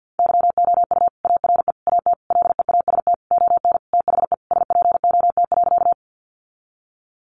Morse code, heard as a keyed tone, about 49 wpm, 700 Hz. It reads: YOU ARE AT LEAST ON THE SPOT.
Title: Sentences from Top 1100 Words 50wpm